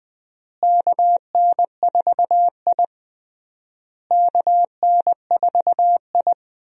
Morse code, heard as a keyed tone, about 20 wpm, 700 Hz. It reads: KN4I KN4I
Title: USA Call Signs - 4 Characters - Repeated 3x - 20wpm